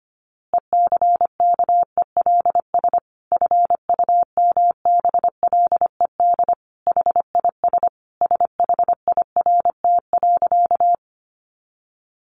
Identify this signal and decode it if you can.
Morse code, heard as a keyed tone, about 25 wpm, 700 Hz. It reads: ECKELS FUMBLED HIS SHIRT.
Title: A Sound Of Thunder 25wpm